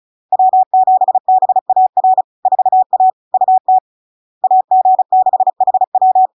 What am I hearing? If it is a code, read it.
Morse code, heard as a keyed tone, about 36 wpm, 750 Hz. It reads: W7BAR 4A UT AG6HW
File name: LARK-FD-2022-Log-in-CW-35wpm-750-Hz